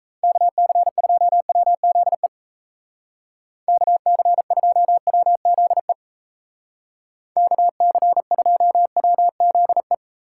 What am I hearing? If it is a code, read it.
Morse code, heard as a keyed tone, about 34 wpm, 700 Hz. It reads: KC2WZE KC2WZE KC2WZE
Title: Speed Racing - USA 2x3 Call Signs 28wpm